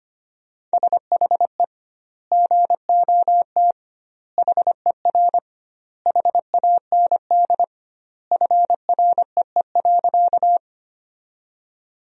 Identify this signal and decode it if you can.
Morse code, heard as a keyed tone, about 25 wpm, 700 Hz. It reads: SHE GOT HER HAND FREE.
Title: 2x Word-Spacing - Sentences from Top 500 Words - 25wpm